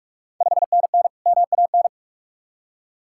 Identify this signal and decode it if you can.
Morse code, heard as a keyed tone, about 44 wpm, 690 Hz. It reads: 5NN MAN
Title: Ontario QSO Party Exchange 45wpm